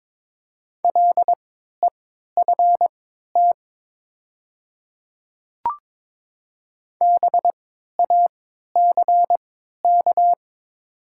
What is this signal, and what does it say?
Morse code, heard as a keyed tone, about 22 wpm, spaced as for 13 wpm, 700 Hz. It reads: LEFT BACK
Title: 3x Character Spacing - Top 500 Words - 22wpm